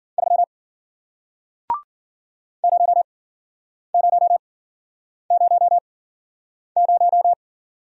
Morse code, heard as a keyed tone, about 55 wpm, spaced as for 9 wpm, 700 Hz. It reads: 3 0000